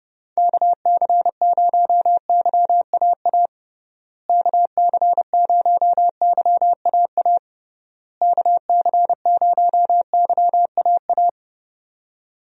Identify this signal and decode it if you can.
Morse code, heard as a keyed tone, about 32 wpm, 700 Hz. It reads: KC0YAA KC0YAA KC0YAA
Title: USA Call Signs - 6 Characters - Repeated 3x - 30wpm